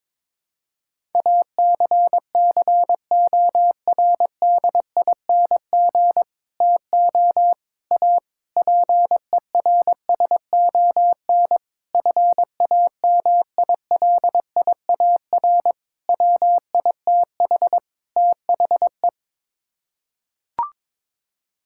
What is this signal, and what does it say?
Morse code, heard as a keyed tone, about 22 wpm, 700 Hz. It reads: ACCORDING TO A PERSON FAMILIAR WITH THE
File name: practice / 22wpm / Sets of 7 Words - 22wpm